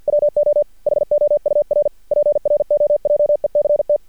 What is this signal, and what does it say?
Morse code, heard as a keyed tone, about 51 wpm, 590 Hz. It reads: WO HOUR PROJECT
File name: archived / CRSnet